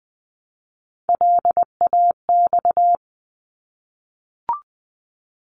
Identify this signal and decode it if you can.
Morse code, heard as a keyed tone, about 20 wpm, 700 Hz. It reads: LAX
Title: ARRL Sections 20wpm